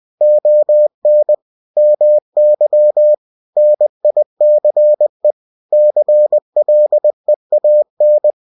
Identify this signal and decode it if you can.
Morse code, heard as a keyed tone, about 20 wpm, 600 Hz. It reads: ON MY NICE CLEAN